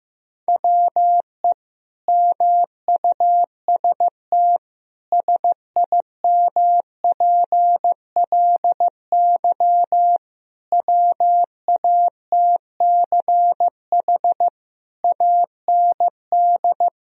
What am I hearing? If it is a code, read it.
Morse code, heard as a keyed tone, about 15 wpm, 700 Hz. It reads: WE MUST SIMPLY WATCH AND
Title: Sentences from Top 1000 Words - Review Part 4 - 15wpm